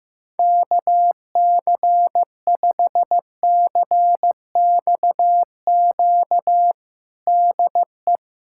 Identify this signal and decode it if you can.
Morse code, heard as a keyed tone, about 15 wpm, 700 Hz. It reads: KC5CXQ DE
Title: USA Callsign DE Callsign 15wpm